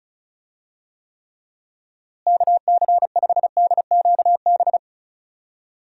Morse code, heard as a keyed tone, about 34 wpm, 700 Hz. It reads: KC5DQB
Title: USA Call Signs - 6 Characters - Repeated 3x - 35wpm